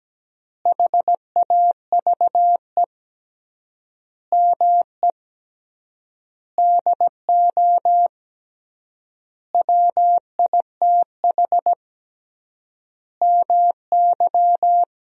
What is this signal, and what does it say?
Morse code, heard as a keyed tone, about 17 wpm, 700 Hz. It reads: HAVE ME DO WITH MY
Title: 3x Word-Spacing - Sentences from Top 300 Words - 17wpm